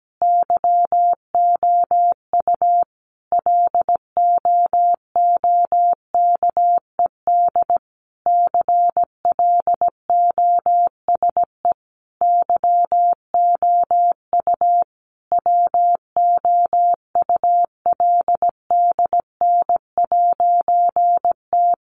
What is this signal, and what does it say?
Morse code, heard as a keyed tone, about 17 wpm, 700 Hz. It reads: YOU LOOKED CLOSE YOU WOULDN'T
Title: A Sound Of Thunder 17wpm